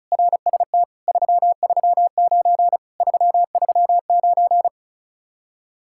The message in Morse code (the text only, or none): RST 339 339